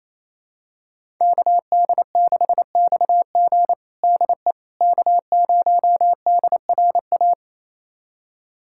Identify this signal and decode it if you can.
Morse code, heard as a keyed tone, about 28 wpm, 700 Hz. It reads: KD6XG DE K0DRA
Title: USA Callsign DE Callsign - 5 Characters - 28wpm